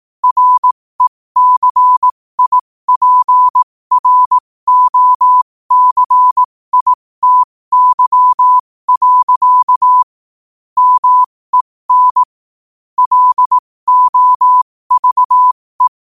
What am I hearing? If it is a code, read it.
Morse code, heard as a keyed tone, about 18 wpm, spaced as for 15 wpm, 1000 Hz. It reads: RECIPROCITY. MEN LOVE
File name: qotdfiles / 15wpm